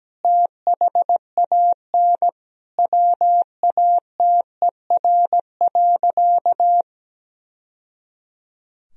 Morse code, heard as a keyed tone, about 17 wpm, 700 Hz. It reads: THAN WATER.
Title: Sentences from Top 1000 Words - Review Part 4 - 17wpm